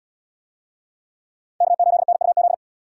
Morse code, heard as a keyed tone, about 77 wpm, 700 Hz. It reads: K8NUZ